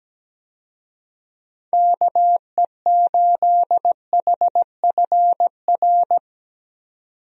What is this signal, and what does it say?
Morse code, heard as a keyed tone, about 17 wpm, 700 Hz. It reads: KE8HFR
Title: USA Call Signs - 6 Characters - 17wpm